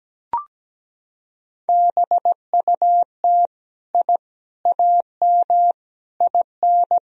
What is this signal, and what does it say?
Morse code, heard as a keyed tone, about 17 wpm, 700 Hz. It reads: BUT I AM IN
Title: Sentences from Top 1000 Words - Review Part 4 - 17wpm